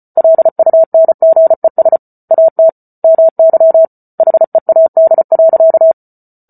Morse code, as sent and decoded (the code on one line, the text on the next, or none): .-.. ..- -. --. ... / .- - / -- -.-- / .... .- -.. .-.-.-
LUNGES AT MY HEAD.